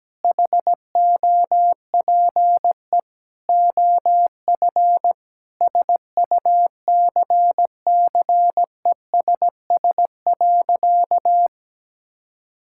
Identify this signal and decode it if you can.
Morse code, heard as a keyed tone, about 17 wpm, 700 Hz. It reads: HOPE OF SUCCESS.
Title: Sentences from Top 1000 Words - Review Part 6 - 17wpm